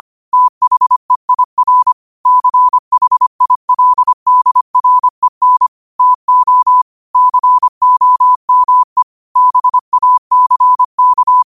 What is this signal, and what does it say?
Morse code, heard as a keyed tone, about 25 wpm, 1000 Hz. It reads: THEIR CHILDREN TO COME BACK
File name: qotdfiles / 25wpm